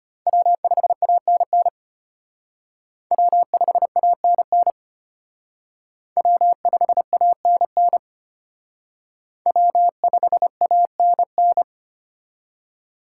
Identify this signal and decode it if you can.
Morse code, heard as a keyed tone, about 32 wpm, 700 Hz. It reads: W5ANN W5ANN W5ANN W5ANN